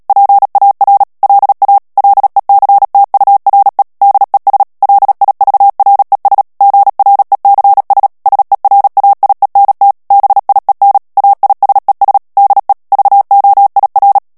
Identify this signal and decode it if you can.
Morse code, heard as a keyed tone, about 37 wpm, 800 Hz. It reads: PAR LA LECTURE DES LIVRES GRECS SERAIENT BIEN AISES DE VOIR
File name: CW / FICHIERS-MP3 / FRANCAIS TEXTE CLAIR 2 VITESSE 35 MOTS MINUTES